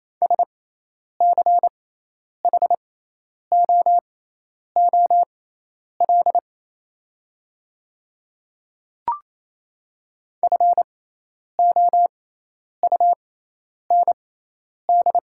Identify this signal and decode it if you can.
Morse code, heard as a keyed tone, about 28 wpm, spaced as for 10 wpm, 700 Hz. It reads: SCHOOL FOUND
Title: 6x Character Spacing - Top 300 Words - 28wpm